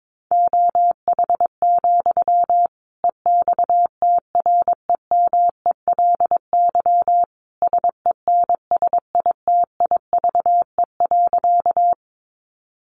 Morse code, heard as a keyed tone, about 22 wpm, 700 Hz. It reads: OH, EXTREMELY SENSITIVE.
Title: Sentences from Top 1400 Words 22wpm